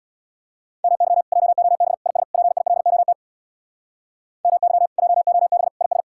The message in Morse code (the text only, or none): MY JOB IS PILOT MY JOB IS